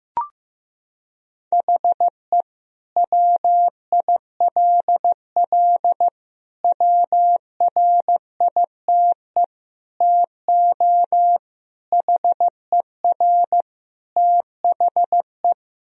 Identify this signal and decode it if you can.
Morse code, heard as a keyed tone, about 15 wpm, 700 Hz. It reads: HE WILL WRITE TO HER THE
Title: Sentences from Top 1000 Words - Review Part 1 - 15wpm